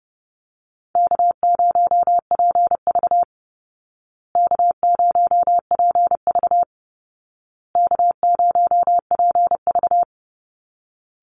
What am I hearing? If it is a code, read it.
Morse code, heard as a keyed tone, about 30 wpm, 700 Hz. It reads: K0PV K0PV K0PV
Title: Call Signs Repeated 2x - 30wpm